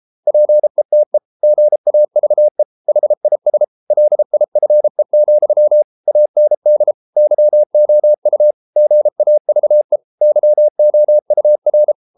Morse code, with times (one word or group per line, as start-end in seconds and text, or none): 0.3-1.2 PETE
1.4-2.6 GAVE
2.9-3.6 HIS
3.9-5.8 LIFE,
6.1-6.9 AND
7.2-8.5 YOU
8.8-10.0 GAVE
10.2-11.9 YOUR